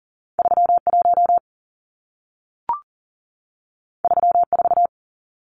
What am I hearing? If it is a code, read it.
Morse code, heard as a keyed tone, about 40 wpm, 700 Hz. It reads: 31 34
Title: Numbers - 2-Digit 40wpm